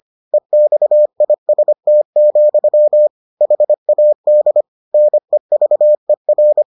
Morse code, heard as a keyed tone, about 25 wpm, 600 Hz. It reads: EXIST, HAD NEVER